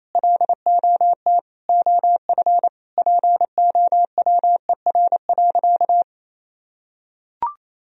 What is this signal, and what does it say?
Morse code, heard as a keyed tone, about 28 wpm, 700 Hz. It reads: LOT OF POWER.
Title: Sentences from Top 1000 Words - Review Part 4 - 28wpm